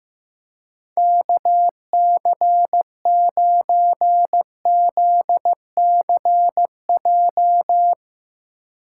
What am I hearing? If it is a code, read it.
Morse code, heard as a keyed tone, about 15 wpm, 700 Hz. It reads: KC9ZCJ
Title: USA Call Signs - 6 Characters - 3x Character Spacing - 15wpm